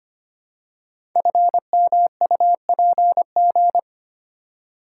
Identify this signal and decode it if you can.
Morse code, heard as a keyed tone, about 25 wpm, 700 Hz. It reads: FMUPG
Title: Code Groups - 5 Letters - 25wpm